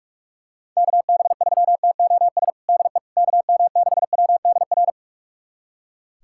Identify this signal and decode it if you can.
Morse code, heard as a keyed tone, about 42 wpm, 700 Hz. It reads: KB3TOS DE KM6WDR
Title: USA Callsign DE Callsign - 6 Characters - 45wpm